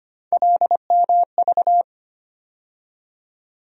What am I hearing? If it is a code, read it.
Morse code, heard as a keyed tone, about 25 wpm, 700 Hz. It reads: LMV
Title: USA Call Sign Postfixes - 3 Letters 25wpm